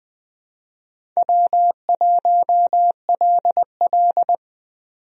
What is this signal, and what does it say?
Morse code, heard as a keyed tone, about 20 wpm, 700 Hz. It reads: W1LL
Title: USA Call Signs - 4 Characters - 20wpm